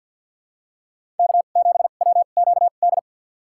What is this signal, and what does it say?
Morse code, heard as a keyed tone, about 47 wpm, spaced as for 35 wpm, 700 Hz. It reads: K7WQD
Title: USA Call Signs - 5 Characters - 2x Character Spacing - 50wpm